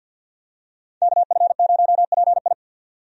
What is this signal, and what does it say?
Morse code, heard as a keyed tone, about 49 wpm, 700 Hz. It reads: KF0PI
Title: USA Call Signs - 5 Characters - 50wpm